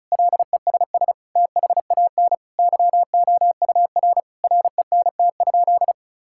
Morse code, heard as a keyed tone, about 34 wpm, 700 Hz. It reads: LESS THAN YOUR RENT?
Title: Sentences from Top 1000 Words - Review Part 5 - 35wpm